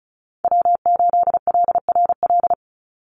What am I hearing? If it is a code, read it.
Morse code, heard as a keyed tone, about 34 wpm, 700 Hz. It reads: W8LRL
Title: USA Call Signs 35wpm